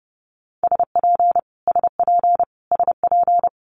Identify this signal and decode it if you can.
Morse code, heard as a keyed tone, about 30 wpm, 700 Hz. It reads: SP SP SP
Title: Multi-letter Phonemes Repeated 3x 30wpm